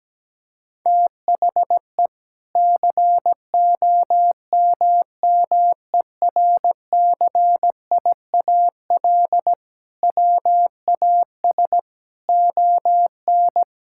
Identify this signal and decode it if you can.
Morse code, heard as a keyed tone, about 17 wpm, 700 Hz. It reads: THE COMMERCIAL WAS ON
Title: Sentences from Top 1000 Words - Review Part 5 - 17wpm